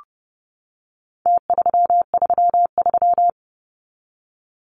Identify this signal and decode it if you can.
Morse code, heard as a keyed tone, about 30 wpm, 700 Hz. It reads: T333
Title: Numbers - Contest Style - 4-Digit 30wpm